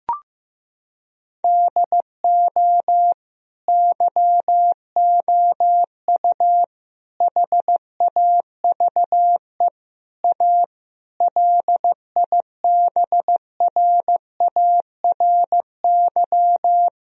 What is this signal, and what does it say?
Morse code, heard as a keyed tone, about 15 wpm, 700 Hz. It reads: DO YOU HAVE A LIBRARY
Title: Sentences from Top 1000 Words - Review Part 4 - 15wpm